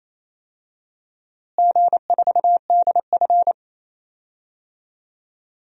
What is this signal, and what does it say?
Morse code, heard as a keyed tone, about 28 wpm, 700 Hz. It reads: G4DF
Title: DXCC-100 Call Signs - 4 Characters - Repeated 2x - 28wpm